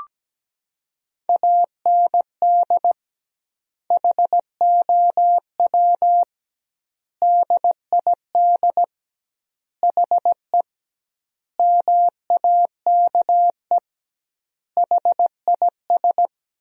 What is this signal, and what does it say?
Morse code, heard as a keyed tone, about 17 wpm, 700 Hz. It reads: AND HOW DID HE MAKE HIS
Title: 2x Word-Spacing - Sentences from Top 400 Words - 17wpm